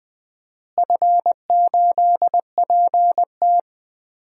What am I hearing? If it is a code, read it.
Morse code, heard as a keyed tone, about 21 wpm, 700 Hz. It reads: F8PT
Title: DXCC-100 Call Signs - 4 Characters - Repeated 3x - 20wpm